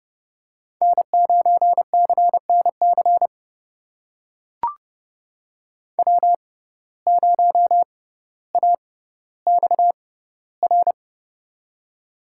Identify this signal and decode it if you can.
Morse code, heard as a keyed tone, about 30 wpm, spaced as for 15 wpm, 700 Hz. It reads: N9CNC W0AXR
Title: USA Call Signs - 5 Characters - 6x Character Spacing - 30wpm